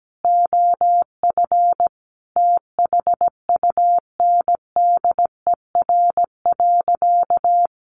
Morse code, heard as a keyed tone, about 17 wpm, 700 Hz. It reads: OF THUNDER.
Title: A Sound Of Thunder 17wpm